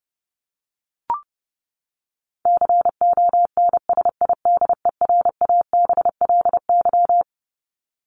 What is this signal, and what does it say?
Morse code, heard as a keyed tone, about 30 wpm, 700 Hz. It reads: CONSIDERABLY
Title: LY Suffix - Top 100 Words with Base Words 30wpm